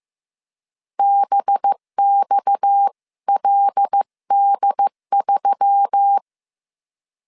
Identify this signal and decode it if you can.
Morse code, heard as a keyed tone, about 15 wpm, 800 Hz. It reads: BXLD3
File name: CW / DET / 1 9 1500C S A 14wpm 3222